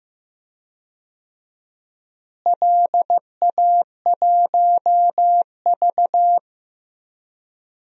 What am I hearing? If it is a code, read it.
Morse code, heard as a keyed tone, about 15 wpm, 700 Hz. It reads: LA1V
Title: DXCC-100 Call Signs - 4 Characters - Repeated 2x - 15wpm